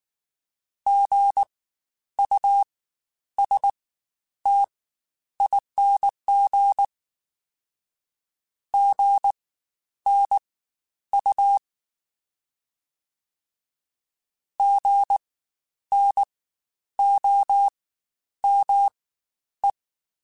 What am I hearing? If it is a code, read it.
Morse code, heard as a keyed tone, about 19 wpm, spaced as for 10 wpm, 800 Hz. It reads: GUSTING GNU GNOME
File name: Spaced Alphas / Spaced Alphas mp3s / Gs sp @18